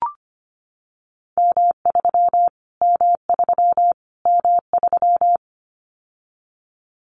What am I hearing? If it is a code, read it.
Morse code, heard as a keyed tone, about 25 wpm, 700 Hz. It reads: M3 M3 M3
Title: Callsign Prefixes for Top 25 DXCC - Repeated 3x - 25wpm